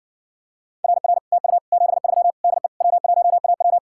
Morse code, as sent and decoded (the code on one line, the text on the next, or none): .-. .-. / - ..- / --... ...-- / -.. / .-- .---- .- .--
RR TU 73 DE W1AW